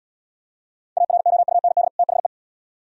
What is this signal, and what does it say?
Morse code, heard as a keyed tone, about 71 wpm, 700 Hz. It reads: AROUND THE